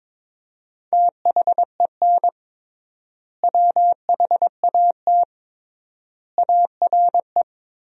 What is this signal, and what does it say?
Morse code, heard as a keyed tone, about 22 wpm, 700 Hz. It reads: THEN WHAT ARE
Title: 3x Word-Spacing - Sentences from Top 100 Words - 22wpm